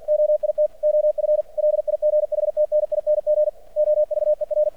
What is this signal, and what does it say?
Morse code, heard as a keyed tone, about 50 wpm, 600 Hz. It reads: OIT OF LIGHTNING OVER
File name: archived / CRSnet